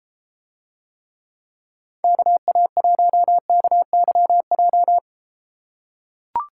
Morse code, heard as a keyed tone, about 33 wpm, 700 Hz. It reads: KA1KYJ